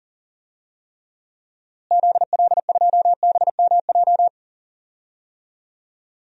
Morse code, heard as a keyed tone, about 40 wpm, 700 Hz. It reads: ZL2BMJ
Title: DXCC-100 Call Signs - 6 Characters - Repeated 3x - 40wpm